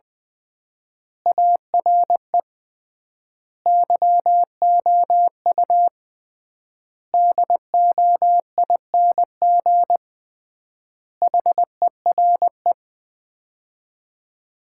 Morse code, heard as a keyed tone, about 20 wpm, 700 Hz. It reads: ARE YOU DOING HERE
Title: 3x Word-Spacing - Sets of 5 Words - 20wpm